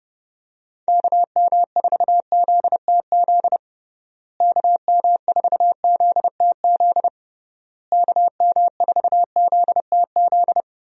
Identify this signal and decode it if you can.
Morse code, heard as a keyed tone, about 30 wpm, 700 Hz. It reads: KM4ZTZ KM4ZTZ KM4ZTZ
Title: USA Call Signs - 6 Characters - Repeated 3x - 30wpm